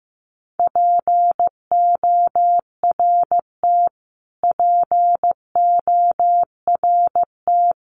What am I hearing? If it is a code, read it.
Morse code, heard as a keyed tone, about 15 wpm, 700 Hz. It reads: PORT PORT
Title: Top 500 Words Repeated 15wpm